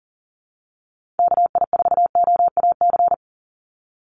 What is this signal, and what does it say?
Morse code, heard as a keyed tone, about 40 wpm, 700 Hz. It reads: KI4OAC
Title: USA Call Signs 40wpm